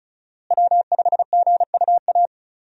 Morse code, heard as a keyed tone, about 36 wpm, 700 Hz. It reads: W5GUA
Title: USA Call Signs - 5 Characters - Repeated 2x - 35wpm